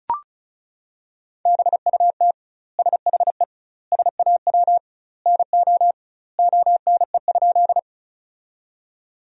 Morse code, heard as a keyed tone, about 34 wpm, 700 Hz. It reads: BUT SHE SAW NO ONE?
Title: 2x Word-Spacing - Sentences from Top 300 Words - 35wpm